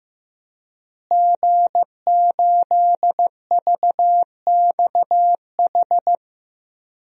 Morse code, heard as a keyed tone, about 15 wpm, 700 Hz. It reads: G8VXH
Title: DXCC-100 Call Signs - 5 Characters - Repeated 2x - 15wpm